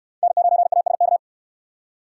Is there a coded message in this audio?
Yes: Morse code from 0.2 to 1.2 s, heard as a keyed tone, about 61 wpm, 700 Hz. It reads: N9ASK